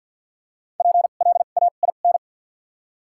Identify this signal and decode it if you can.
Morse code, heard as a keyed tone, about 53 wpm, spaced as for 37 wpm, 700 Hz. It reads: PLAIN
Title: 2x Character Spacing - Top 500 Words - 50wpm